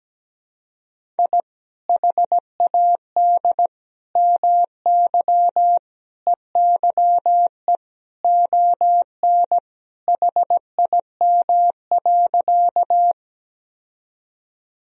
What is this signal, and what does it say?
Morse code, heard as a keyed tone, about 17 wpm, 700 Hz. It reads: I HAD MY EYE ON HIM.
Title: Sentences from Top 1000 Words - Review Part 1 - 17wpm